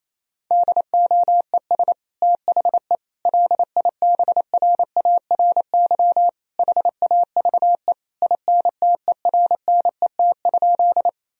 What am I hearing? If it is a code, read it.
Morse code, heard as a keyed tone, about 27 wpm, 700 Hz. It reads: DOES THE LIBRARY HAVE INTERNET?
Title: Sentences from Top 1000 Words - Review Part 4 - 28wpm